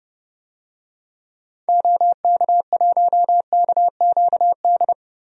Morse code, heard as a keyed tone, about 29 wpm, 700 Hz. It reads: OK1KQD
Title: DXCC-100 Call Signs - 6 Characters - 30wpm